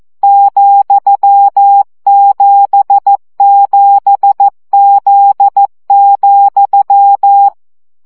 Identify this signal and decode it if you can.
Morse code, heard as a keyed tone, about 14 wpm, 800 Hz. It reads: ,77Z,
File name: CW gyakorlas z 7 vesszo 12 baud